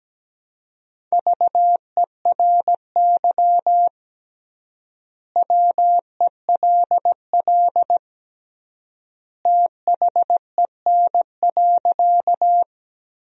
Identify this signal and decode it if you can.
Morse code, heard as a keyed tone, about 17 wpm, 700 Hz. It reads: VERY WELL THEN.
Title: 3x Word-Spacing - Sentences from Top 200 Words - 17wpm